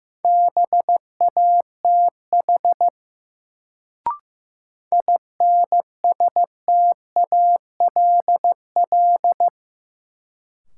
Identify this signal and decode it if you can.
Morse code, heard as a keyed tone, about 15 wpm, 700 Hz. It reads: BATH INSTALL
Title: Sentences from Top 1600 Words - New Word Drill Alt - 15wpm